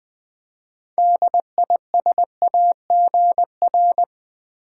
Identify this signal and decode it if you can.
Morse code, heard as a keyed tone, about 20 wpm, 700 Hz. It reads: DISAGR
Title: Build A Word - 6 Letters At A Time - 12-Letter Words - 20wpm